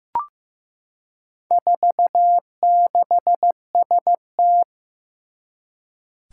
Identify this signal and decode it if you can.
Morse code, heard as a keyed tone, about 15 wpm, 700 Hz. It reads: 46ST
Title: Four Challenging Letter-Number Combinations 15wpm